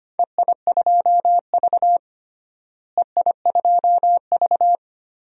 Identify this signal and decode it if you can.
Morse code, heard as a keyed tone, about 25 wpm, 700 Hz. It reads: EI2V EI2V
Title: DXCC-100 Call Signs - 4 Characters - Repeated 2x - 25wpm